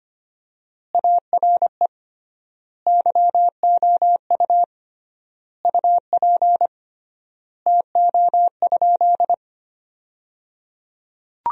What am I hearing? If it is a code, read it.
Morse code, heard as a keyed tone, about 25 wpm, 700 Hz. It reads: ARE YOU UP TO?
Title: 3x Word-Spacing - Sentences from Top 100 Words - 25wpm